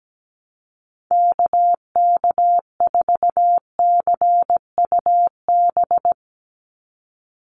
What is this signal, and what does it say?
Morse code, heard as a keyed tone, about 17 wpm, 700 Hz. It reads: KK4CUB
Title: USA Call Signs 17wpm